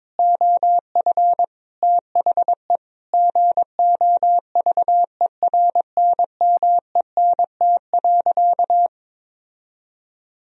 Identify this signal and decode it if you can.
Morse code, heard as a keyed tone, about 22 wpm, 700 Hz. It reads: OF THE GOVERNMENT.
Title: Sentences from Top 1000 Words - Review Part 3 - 22wpm